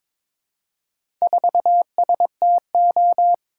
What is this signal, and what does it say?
Morse code, heard as a keyed tone, about 22 wpm, 700 Hz. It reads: 4STO
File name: practice / 22wpm / Four Letter-Number Combinations - 22wpm